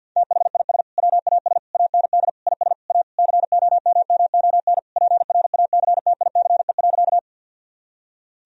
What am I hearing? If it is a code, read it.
Morse code, heard as a keyed tone, about 47 wpm, 700 Hz. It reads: THIS WAS AND IS A COMMON PRACTICE.
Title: Sentences from Top 1000 Words - Review Part 4 - 50wpm